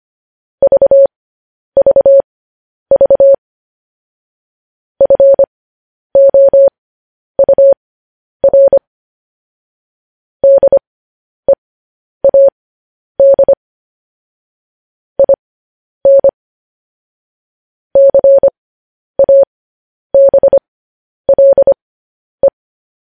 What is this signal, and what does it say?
Morse code, heard as a keyed tone, about 25 wpm, spaced as for 10 wpm, 550 Hz. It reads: VVV FOUR DEAD IN CABLE
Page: news headlines in morse code at 10 wpm